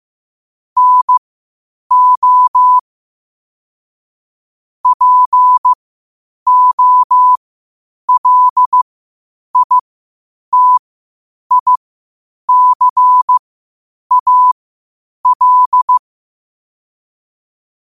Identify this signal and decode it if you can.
Morse code, heard as a keyed tone, about 15 wpm, spaced as for 8 wpm, 1000 Hz. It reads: NO POLITICAL
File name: qotdfiles / 8wpm